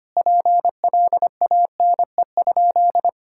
Morse code, heard as a keyed tone, about 25 wpm, 700 Hz. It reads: PLANE?
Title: Sentences from Top 1000 Words - Review Part 6 - 25wpm